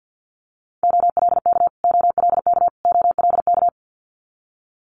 Morse code, heard as a keyed tone, about 50 wpm, 700 Hz. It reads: OLK OLK OLK
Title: Multi-letter Phonemes Repeated 3x 50wpm